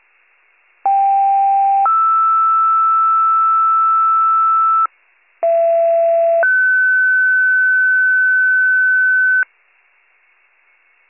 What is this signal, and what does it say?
Two-tone page, heard as a one-second tone, then a three-second tone, about 790 then 1320 Hz, about 670 then 1500 Hz. It is heard from 0.9 to 9.4 s.